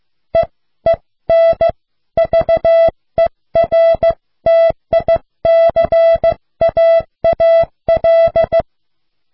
Morse code, heard as a keyed tone, about 15 wpm, 660 Hz. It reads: EEN VERTICAAL